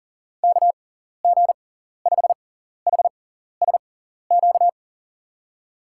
Morse code, heard as a keyed tone, about 37 wpm, spaced as for 14 wpm, 700 Hz. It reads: KG5HSQ
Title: USA Call Signs - 6 Characters - 6x Character Spacing - 40wpm